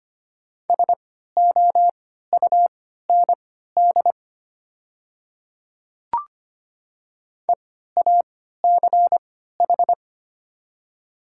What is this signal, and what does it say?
Morse code, heard as a keyed tone, about 25 wpm, spaced as for 14 wpm, 690 Hz. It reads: SOUND EACH
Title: 3x Character Spacing - Top 100 Words - 25wpm